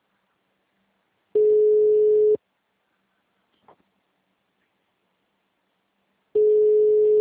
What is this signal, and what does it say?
Ringback tone (Europe, 425 Hz).
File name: out- -2014- - - .46